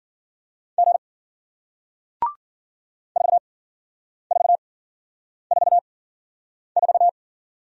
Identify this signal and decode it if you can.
Morse code, heard as a keyed tone, about 50 wpm, spaced as for 9 wpm, 700 Hz. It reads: G 4444